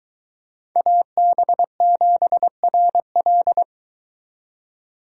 Morse code, heard as a keyed tone, about 22 wpm, 700 Hz. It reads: AB7RL